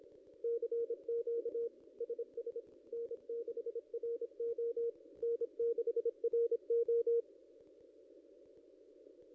Morse code, heard as a keyed tone, about 26 wpm, 470 Hz. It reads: CQ SS N6RO N6RO
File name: contests / ss-cw-k5tr-op-n5rz -0900